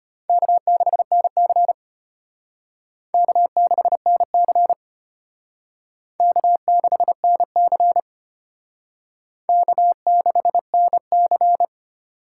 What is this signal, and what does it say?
Morse code, heard as a keyed tone, about 32 wpm, 700 Hz. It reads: K6NC K6NC K6NC K6NC